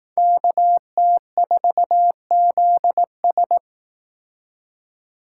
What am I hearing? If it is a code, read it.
Morse code, heard as a keyed tone, about 18 wpm, 700 Hz. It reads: KT4ZS
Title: Speed Racing - USA 2x2 Call Signs 15wpm